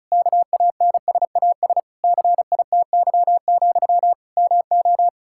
Morse code, heard as a keyed tone, about 36 wpm, 700 Hz. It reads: KANSAS CITY, MO
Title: K1USN SST QSO Practice 35wpm